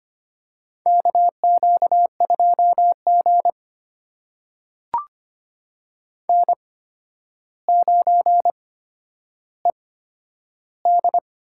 Morse code, heard as a keyed tone, about 25 wpm, spaced as for 11 wpm, 700 Hz. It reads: KQ2G N9ED
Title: USA Call Signs - 4 Characters - 8x Character Spacing - 25wpm